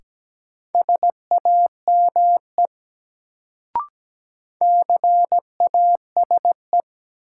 Morse code, heard as a keyed tone, about 17 wpm, 700 Hz. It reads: SAME CASE
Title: Sets of 2 Words - New Word Drill - 17wpm